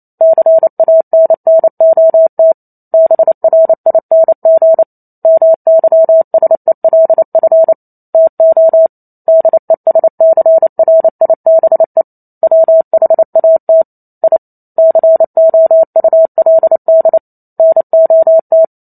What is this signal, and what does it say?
Morse code, heard as a keyed tone, about 28 wpm, 650 Hz. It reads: CANNOT BRING MYSELF TO DESCRIBE WHAT I COULD NOT